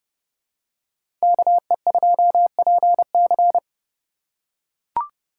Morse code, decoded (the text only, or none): KE2PC